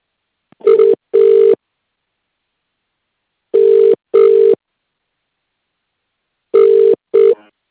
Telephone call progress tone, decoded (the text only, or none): ringback tone